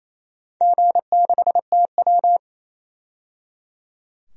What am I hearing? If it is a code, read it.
Morse code, heard as a keyed tone, about 28 wpm, 700 Hz. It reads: G6TW
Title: DXCC-100 Call Signs - 4 Characters - 28wpm